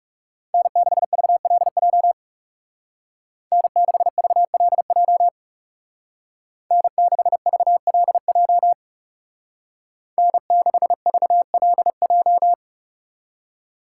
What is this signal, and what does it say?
Morse code, heard as a keyed tone, about 35 wpm, 700 Hz. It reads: N6VLJ N6VLJ N6VLJ N6VLJ
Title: Speed Racing - USA 1x3 Call Signs 30wpm